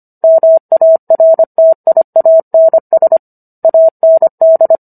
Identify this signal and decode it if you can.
Morse code, heard as a keyed tone, about 25 wpm, 650 Hz. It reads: MARTIANS AND